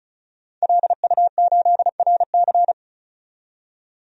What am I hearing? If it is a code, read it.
Morse code, heard as a keyed tone, about 34 wpm, 700 Hz. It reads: LU8RC
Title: DXCC-100 Call Signs - 5 Characters - 35wpm